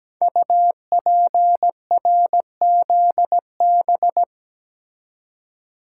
Morse code, heard as a keyed tone, about 17 wpm, 700 Hz. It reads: UPRZB
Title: Code Groups - 5 Letters - 17wpm